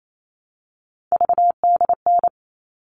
Morse code, heard as a keyed tone, about 28 wpm, 700 Hz. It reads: VDN
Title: All Cut Numbers - 3-Digit 28wpm